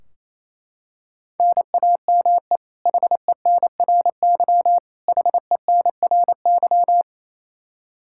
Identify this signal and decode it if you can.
Morse code, heard as a keyed tone, about 28 wpm, 700 Hz. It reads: NAME HENRY HENRY
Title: Practice QSO - Exchange 1 - Beginning - 28wpm